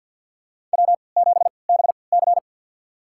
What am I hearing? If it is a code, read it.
Morse code, heard as a keyed tone, about 47 wpm, spaced as for 28 wpm, 700 Hz. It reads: W7BC
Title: USA Call Signs - 4 Characters - 3x Character Spacing - 50wpm